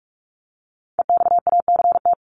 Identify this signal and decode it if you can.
Morse code, heard as a keyed tone, about 45 wpm, 700 Hz. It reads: EXACT
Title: LY Suffix - Top 100 Words with Base Words 45wpm